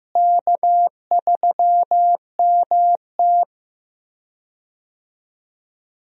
Morse code, heard as a keyed tone, about 15 wpm, 700 Hz. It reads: K3MT
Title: USA Call Signs - 4 Characters - Repeated 2x - 15wpm